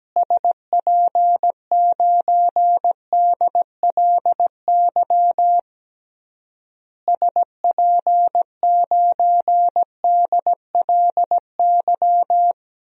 Morse code, heard as a keyed tone, about 17 wpm, 700 Hz. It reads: SP9DLY SP9DLY
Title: DXCC-100 Call Signs - 6 Characters - Repeated 3x - 17wpm